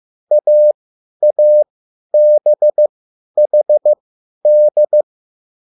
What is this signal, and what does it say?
Morse code, heard as a keyed tone, about 15 wpm, spaced as for 10 wpm, 600 Hz. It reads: AABHD